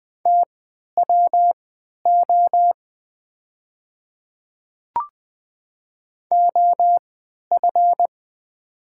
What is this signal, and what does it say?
Morse code, heard as a keyed tone, about 20 wpm, spaced as for 11 wpm, 700 Hz. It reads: TWO OF
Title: 3x Character Spacing - Top 100 Words - 20wpm